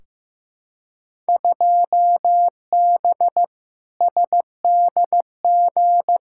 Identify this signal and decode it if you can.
Morse code, heard as a keyed tone, about 15 wpm, 700 Hz. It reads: 2B SDG
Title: Field Day Exchanges Encore - 15wpm